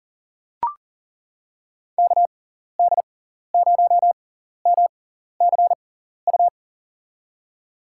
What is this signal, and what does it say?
Morse code, heard as a keyed tone, about 40 wpm, spaced as for 14 wpm, 700 Hz. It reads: KD0MCU
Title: USA Call Signs - 6 Characters - 6x Character Spacing - 40wpm